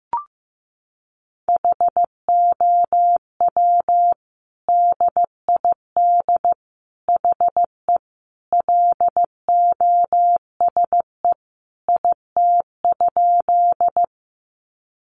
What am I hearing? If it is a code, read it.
Morse code, heard as a keyed tone, about 15 wpm, 700 Hz. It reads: HOW DID HE LOSE IT?
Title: Sentences from Top 1100 Words 15wpm